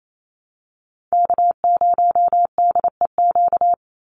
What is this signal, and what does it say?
Morse code, heard as a keyed tone, about 28 wpm, 700 Hz. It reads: K0DEQ
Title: Call Signs Repeated 2x - 28wpm